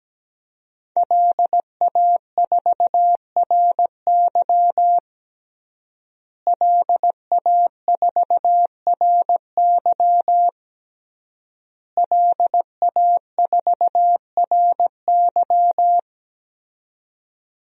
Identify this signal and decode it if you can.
Morse code, heard as a keyed tone, about 17 wpm, 700 Hz. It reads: LA4RY LA4RY LA4RY
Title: DXCC-100 Call Signs - 5 Characters - Repeated 3x - 17wpm